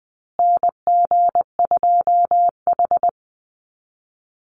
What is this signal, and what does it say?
Morse code, heard as a keyed tone, about 20 wpm, 700 Hz. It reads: NG2H
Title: Call Signs Repeated 3x - 20wpm